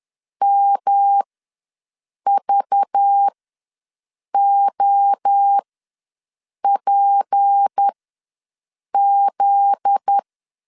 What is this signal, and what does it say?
Morse code, heard as a keyed tone, about 10 wpm, spaced as for 6 wpm, 800 Hz. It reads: MVOPZ
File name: LEZIONI / 10 SIGNS / 10B INT ws 10wpm 4134